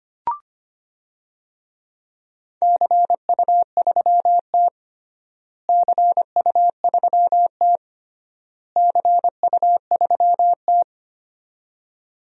Morse code, heard as a keyed tone, about 25 wpm, 700 Hz. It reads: CU3T CU3T CU3T
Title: DXCC-100 Call Signs - 4 Characters - Repeated 3x - 25wpm